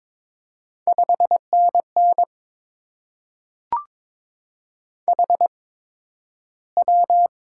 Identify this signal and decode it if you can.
Morse code, heard as a keyed tone, about 22 wpm, 700 Hz. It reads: 5 N N H W
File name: Top 20 QSO Elements - 8x Character Spacing - 22wpm